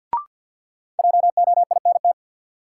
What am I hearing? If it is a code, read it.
Morse code, heard as a keyed tone, about 53 wpm, 700 Hz. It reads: JOINT
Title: Sentences from Top 1300 Words - New Word Drill Alt - 50wpm